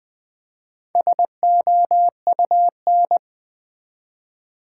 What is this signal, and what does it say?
Morse code, heard as a keyed tone, about 20 wpm, 700 Hz. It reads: SOUN
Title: Build A Word - 1 Letter At A Time - 5-Letter Words - 20wpm